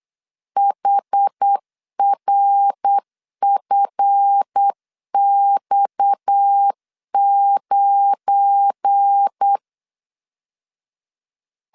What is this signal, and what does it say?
Morse code, heard as a keyed tone, about 8 wpm, 800 Hz. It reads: HRFX9
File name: LEZIONI / 7 GF9Z / 7 GF9 8wpm 3545